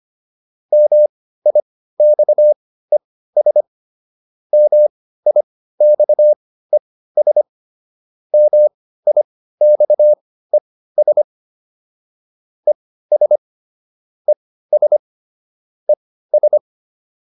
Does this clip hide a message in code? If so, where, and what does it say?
Yes: Morse code from 0.7 to 16.6 s, heard as a keyed tone, about 25 wpm, spaced as for 15 wpm, 600 Hz. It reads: MIXES MIXES MIXES ES ES ES